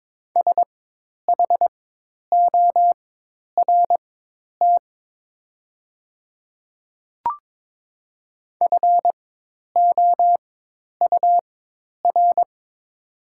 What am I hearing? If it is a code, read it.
Morse code, heard as a keyed tone, about 22 wpm, spaced as for 10 wpm, 700 Hz. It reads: SHORT FOUR
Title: 4x Character Spacing - Top 500 Words - 22wpm